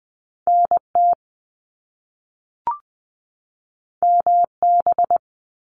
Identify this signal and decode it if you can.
Morse code, heard as a keyed tone, about 20 wpm, 700 Hz. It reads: NT MB
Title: Canadian Provinces and Territories 20wpm